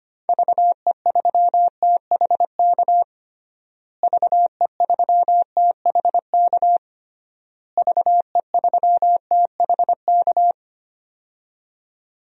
Morse code, heard as a keyed tone, about 25 wpm, 700 Hz. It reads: VE3THK VE3THK VE3THK
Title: DXCC-100 Call Signs - 6 Characters - Repeated 3x - 25wpm